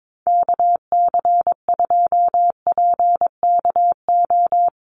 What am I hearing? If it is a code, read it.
Morse code, heard as a keyed tone, about 22 wpm, 700 Hz. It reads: KC2PKO
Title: USA Call Signs 22wpm